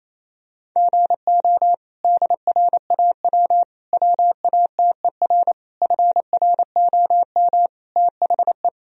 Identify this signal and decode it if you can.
Morse code, heard as a keyed tone, about 28 wpm, 700 Hz. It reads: GO DRAW WATER FROM THE
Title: Sentences from Top 1000 Words - Review Part 1 - 28wpm